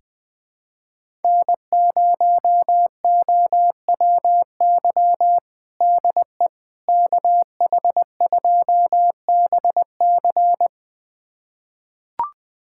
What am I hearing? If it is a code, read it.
Morse code, heard as a keyed tone, about 21 wpm, 700 Hz. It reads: N0OWY DE KH2BC
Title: USA Callsign DE Callsign - 5 Characters - 20wpm